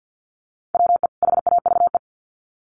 Morse code, heard as a keyed tone, about 50 wpm, 700 Hz. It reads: WE HAVE
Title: Sets of 2 Words 50wpm